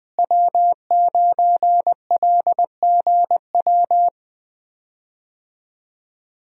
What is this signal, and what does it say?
Morse code, heard as a keyed tone, about 20 wpm, 700 Hz. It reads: W9LGW
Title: USA Call Signs - 5 Characters - Repeated 2x - 20wpm